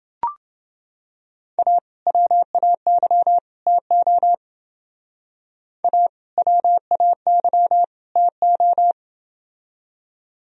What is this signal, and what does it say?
Morse code, heard as a keyed tone, about 30 wpm, 700 Hz. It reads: A WAY TO A WAY TO